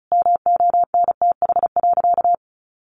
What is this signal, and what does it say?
Morse code, heard as a keyed tone, about 34 wpm, 700 Hz. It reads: MONTH.
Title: Sentences from Top 800 Words 35wpm